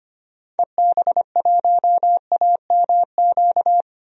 Morse code, heard as a keyed tone, about 25 wpm, 700 Hz. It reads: EB1AMQ
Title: DXCC-100 Call Signs - 6 Characters - 25wpm